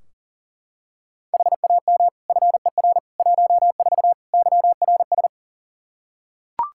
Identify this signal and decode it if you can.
Morse code, heard as a keyed tone, about 37 wpm, 700 Hz. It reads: HAM FER 14 YRS
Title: Practice QSO - Exchange 3 - More Info - 40wpm